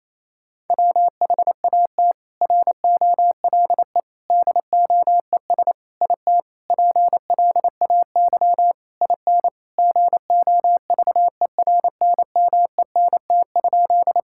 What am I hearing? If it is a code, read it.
Morse code, heard as a keyed tone, about 28 wpm, 700 Hz. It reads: WHAT ROLE DOES IT PLAY IN GOVERNMENT?
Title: Sentences from Top 1000 Words - Review Part 6 - 28wpm